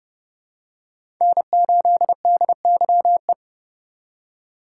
Morse code, heard as a keyed tone, about 30 wpm, 700 Hz. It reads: N8DYE
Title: USA Call Signs - 5 Characters - 4x Character Spacing - 30wpm